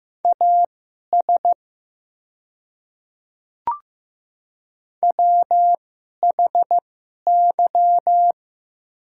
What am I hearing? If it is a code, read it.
Morse code, heard as a keyed tone, about 15 wpm, spaced as for 11 wpm, 700 Hz. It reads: AS WHY
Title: 2x Character Spacing - Top 300 Words - 15wpm